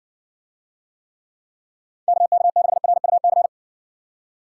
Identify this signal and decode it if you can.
Morse code, heard as a keyed tone, about 60 wpm, 700 Hz. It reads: KK6RUC